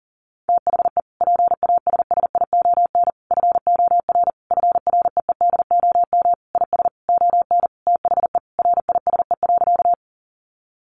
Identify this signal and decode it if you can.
Morse code, heard as a keyed tone, about 40 wpm, 700 Hz. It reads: THE PASSION FOR FREEDOM IS ON THE RISE.
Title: Sentences from Top 1300 Words 40wpm